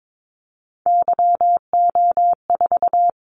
Morse code, heard as a keyed tone, about 22 wpm, 700 Hz. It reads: YO4
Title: Callsign Prefixes for Top 25 DXCC - 22wpm